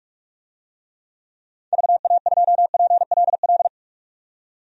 Morse code, heard as a keyed tone, about 46 wpm, 700 Hz. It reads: VA2PLL
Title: DXCC-100 Call Signs - 6 Characters - 45wpm